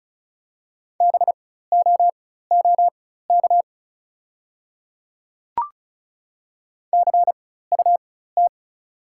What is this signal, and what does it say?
Morse code, heard as a keyed tone, about 34 wpm, spaced as for 16 wpm, 690 Hz. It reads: BOOK CUT
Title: 4x Character Spacing - Top 500 Words - 35wpm